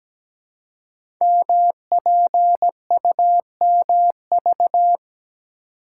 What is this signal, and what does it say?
Morse code, heard as a keyed tone, about 17 wpm, 700 Hz. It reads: MPUMV
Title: Code Groups - 5 Letters - 17wpm